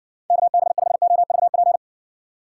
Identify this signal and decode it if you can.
Morse code, heard as a keyed tone, about 63 wpm, 700 Hz. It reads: KB5OVP